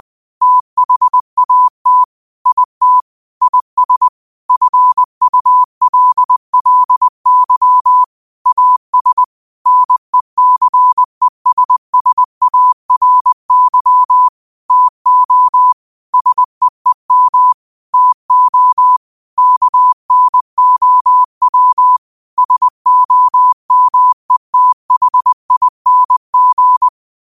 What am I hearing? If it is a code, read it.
Morse code, heard as a keyed tone, about 20 wpm, 1000 Hz. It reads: THAT IT IS FULLY AS NECESSARY TO SEEM TO KNOW SOMETHING
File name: qotdfiles / 20wpm